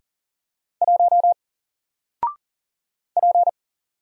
Morse code, heard as a keyed tone, about 40 wpm, 700 Hz. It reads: 1 P
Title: Single Letter-Number Alt 40wpm